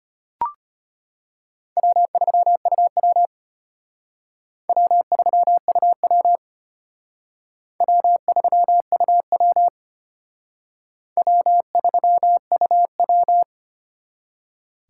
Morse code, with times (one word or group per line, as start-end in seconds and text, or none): 1.8-3.3 W3UW
4.7-6.4 W3UW
7.8-9.7 W3UW
11.2-13.4 W3UW